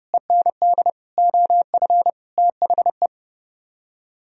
Morse code, heard as a keyed tone, about 35 wpm, 700 Hz. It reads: END OF THE
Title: Speed-Racing - Sets of 5 Words 30wpm